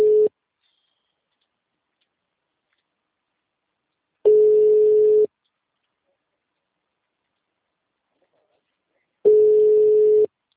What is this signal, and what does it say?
Ringback tone (Europe, 425 Hz).
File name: out- -2002